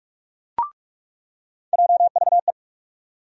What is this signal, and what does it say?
Morse code, heard as a keyed tone, about 48 wpm, 700 Hz. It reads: JVE
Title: USA Call Sign Postfixes - 3 Letters 45wpm